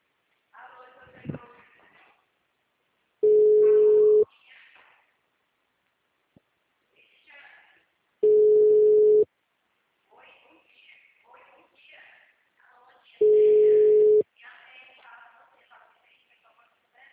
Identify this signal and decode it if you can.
Ringback tone (Europe, 425 Hz), with speech.
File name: out- -2039- - - .19586